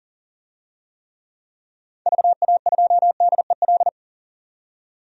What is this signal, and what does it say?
Morse code, heard as a keyed tone, about 37 wpm, 700 Hz. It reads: VA2DEL
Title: DXCC-100 Call Signs - 6 Characters - 40wpm